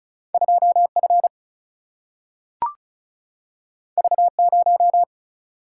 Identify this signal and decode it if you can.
Morse code, heard as a keyed tone, about 36 wpm, 700 Hz. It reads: 2F V0
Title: Single-Letter Single-Number 35wpm